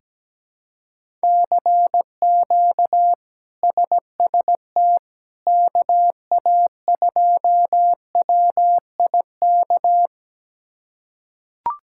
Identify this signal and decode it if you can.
Morse code, heard as a keyed tone, about 17 wpm, 700 Hz. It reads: CQ SST KA2WIK
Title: K1USN SST QSO Practice - 17wpm